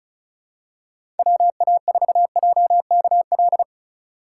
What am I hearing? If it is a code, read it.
Morse code, heard as a keyed tone, about 34 wpm, 700 Hz. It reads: WA4JKL
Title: K1USN SST QSO Practice - 35wpm